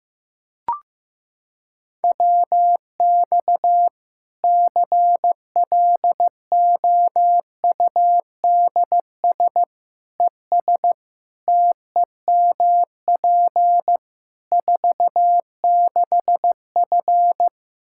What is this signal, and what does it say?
Morse code, heard as a keyed tone, about 16 wpm, 700 Hz. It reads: WX CLOUDS ES TEMP 46F
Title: Practice QSO - Exchange 2 - Info - 15wpm